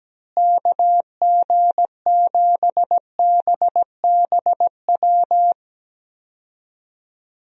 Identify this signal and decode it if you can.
Morse code, heard as a keyed tone, about 17 wpm, 700 Hz. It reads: KG7BBW
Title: USA Call Signs - 6 Characters - Repeated 3x - 17wpm